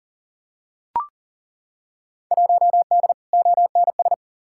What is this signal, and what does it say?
Morse code, heard as a keyed tone, about 40 wpm, 700 Hz. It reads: 1D ONS
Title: Field Day Exchanges - 40wpm